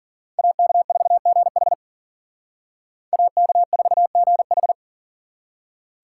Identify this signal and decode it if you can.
Morse code, heard as a keyed tone, about 51 wpm, 700 Hz. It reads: AK4GH AK4GH